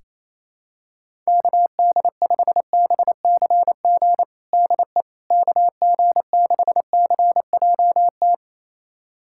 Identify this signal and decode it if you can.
Morse code, heard as a keyed tone, about 28 wpm, 700 Hz. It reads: KD5BCG DE KG6CJT
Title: USA Callsign DE Callsign 28wpm